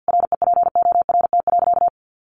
Morse code, heard as a keyed tone, about 50 wpm, 700 Hz. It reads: REPORT.
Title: Sentences from Top 1300 Words 50wpm